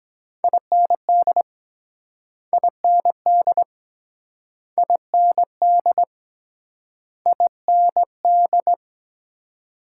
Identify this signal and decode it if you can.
Morse code, heard as a keyed tone, about 23 wpm, 700 Hz. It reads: IND IND IND IND